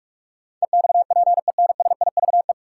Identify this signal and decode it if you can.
Morse code, heard as a keyed tone, about 42 wpm, 700 Hz. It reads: EXPENSIVE
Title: Sentences from Top 1500 Words - New Word Drill Alt - 45wpm